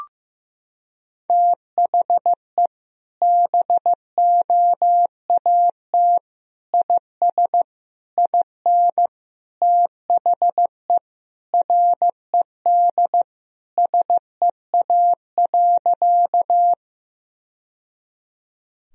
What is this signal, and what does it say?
Morse code, heard as a keyed tone, about 15 wpm, 700 Hz. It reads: THE BOAT IS IN THE RED SEA.
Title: Sentences from Top 1000 Words - Review Part 2 - 15wpm